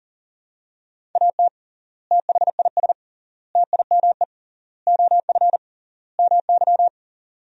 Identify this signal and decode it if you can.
Morse code, heard as a keyed tone, about 40 wpm, 700 Hz. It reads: AT THIS TIME OF MY
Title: 3x Word-Spacing - Sentences from Top 300 Words - 40wpm